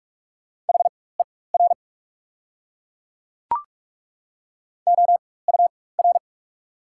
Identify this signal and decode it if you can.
Morse code, heard as a keyed tone, about 42 wpm, spaced as for 21 wpm, 690 Hz. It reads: HER OUR
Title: 4x Character Spacing - Top 200 Words - 45wpm